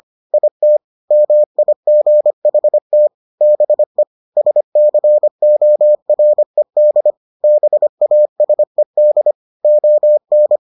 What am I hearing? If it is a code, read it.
Morse code, heard as a keyed tone, about 25 wpm, 600 Hz. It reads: IT MIGHT BE SCORED BASED ON